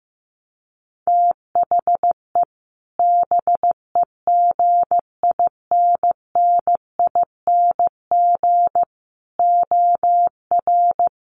Morse code, heard as a keyed tone, about 15 wpm, 700 Hz. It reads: THE BEGINNING OR
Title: Sentences from Top 800 Words 15wpm